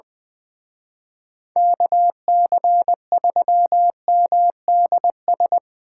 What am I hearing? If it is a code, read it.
Morse code, heard as a keyed tone, about 24 wpm, 700 Hz. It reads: KC3MDS
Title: Speed Racing - USA 2x3 Call Signs 20wpm